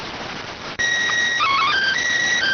SSTV (slow-scan television), PD 290.